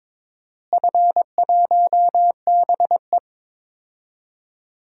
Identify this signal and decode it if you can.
Morse code, heard as a keyed tone, about 21 wpm, 700 Hz. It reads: F1BE
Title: DXCC-100 Call Signs - 4 Characters - Repeated 2x - 22wpm